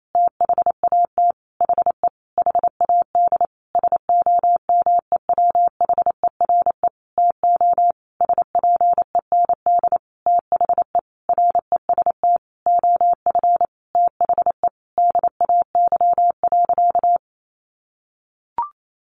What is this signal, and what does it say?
Morse code, heard as a keyed tone, about 28 wpm, 700 Hz. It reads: THAT HE HAD SOMEWHERE TO SPEND THE REST OF THE DAY.
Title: Sentences from Top 1400 Words 28wpm